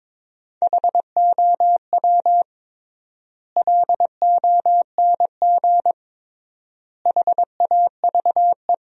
Morse code, heard as a keyed tone, about 22 wpm, 700 Hz. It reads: HOW LONG HAVE
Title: 3x Word-Spacing - Sentences from Top 200 Words - 22wpm